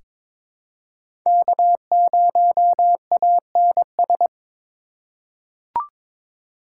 Morse code, heard as a keyed tone, about 23 wpm, 700 Hz. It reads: K0ANS
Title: USA Call Signs - 5 Characters - 2x Character Spacing - 22wpm